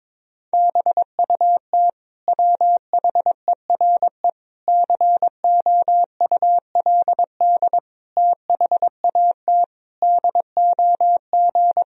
Morse code, heard as a keyed tone, about 22 wpm, 700 Hz. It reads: BUT WHERE COULD THAT DOG
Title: Sentences from Top 1000 Words - Review Part 2 - 22wpm